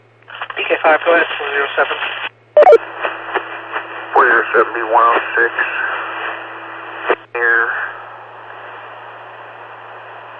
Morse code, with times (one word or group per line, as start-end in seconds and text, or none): none